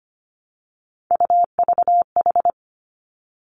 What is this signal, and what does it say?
Morse code, heard as a keyed tone, about 25 wpm, 700 Hz. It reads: UVH
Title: Three Challenging Letter-Number Combinations 25wpm